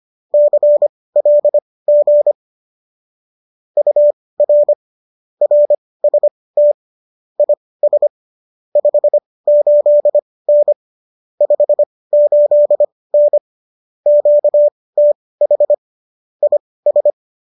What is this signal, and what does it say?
Morse code, heard as a keyed tone, about 25 wpm, spaced as for 18 wpm, 590 Hz. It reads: CLG UR RST IS 58N 58N QTH IS
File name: qso215 18